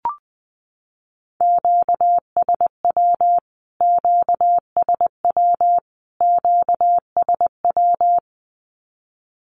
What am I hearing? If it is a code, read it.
Morse code, heard as a keyed tone, about 20 wpm, 700 Hz. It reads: QSW QSW QSW
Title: Q-Codes Repeated 3x 20wpm